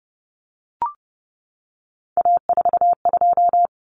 Morse code, heard as a keyed tone, about 28 wpm, 700 Hz. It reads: A42
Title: Common Cut Numbers - 3-Digit 30wpm